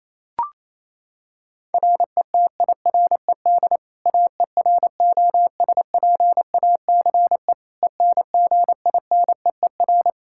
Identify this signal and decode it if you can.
Morse code, heard as a keyed tone, about 28 wpm, 700 Hz. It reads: RETIRED AEROSPACE ENGINEER
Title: K1USN SST QSO Practice 28wpm